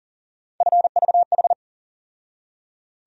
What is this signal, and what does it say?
Morse code, heard as a keyed tone, about 40 wpm, 700 Hz. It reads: FVH
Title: USA Call Sign Postfixes - 3 Letters 40wpm